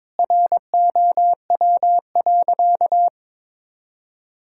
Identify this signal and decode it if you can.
Morse code, heard as a keyed tone, about 22 wpm, 700 Hz. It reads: ROW.
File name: Sentences with 7-Word Phrases - 22wpm